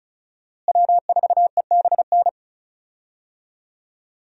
Morse code, heard as a keyed tone, about 34 wpm, 700 Hz. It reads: W4EBN
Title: USA Call Signs - 5 Characters - Repeated 2x - 35wpm